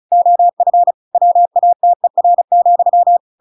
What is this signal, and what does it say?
Morse code, heard as a keyed tone, about 34 wpm, 700 Hz. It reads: OF WATER,
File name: Pinocchio / Pinocchio ch8 35 35 ik1hge